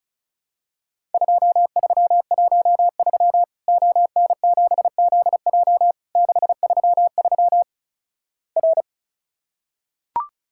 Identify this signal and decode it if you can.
Morse code, heard as a keyed tone, about 36 wpm, 700 Hz. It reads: 2313 ON7ZJ 633 R
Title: Worked All Europe DX Contest - QTC Exchange - 35wpm